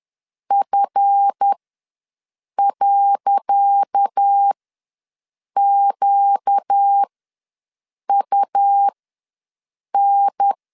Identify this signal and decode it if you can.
Morse code, heard as a keyed tone, about 10 wpm, spaced as for 6 wpm, 800 Hz. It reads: F.QUN
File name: LEZIONI / 10 SIGNS / 10C DOT ws 10wpm 4213